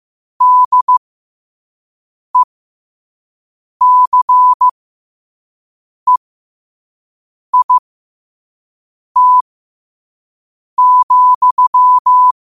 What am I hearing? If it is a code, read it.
Morse code, heard as a keyed tone, about 15 wpm, spaced as for 5 wpm, 1000 Hz. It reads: DECEIT,
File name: qotdfiles / 5wpm